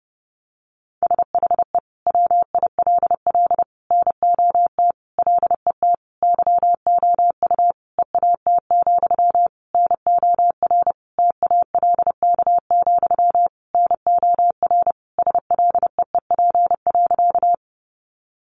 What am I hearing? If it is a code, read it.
Morse code, heard as a keyed tone, about 30 wpm, 700 Hz. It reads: SHE WILL NOT LET YOU EAT, NOR TALK, NOR SLEEP.
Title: Sentences from Top 1500 Words 30wpm